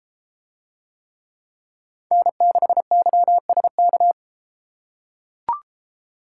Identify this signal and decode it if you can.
Morse code, heard as a keyed tone, about 33 wpm, 700 Hz. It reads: N6YSK